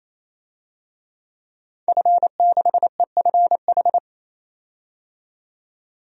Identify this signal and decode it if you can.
Morse code, heard as a keyed tone, about 28 wpm, 700 Hz. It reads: F6EFH
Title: DXCC-100 Call Signs - 5 Characters - 28wpm